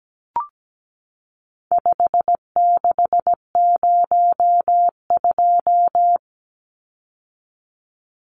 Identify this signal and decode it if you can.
Morse code, heard as a keyed tone, about 17 wpm, 700 Hz. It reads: 5602
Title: Numbers - 4-Digit 17wpm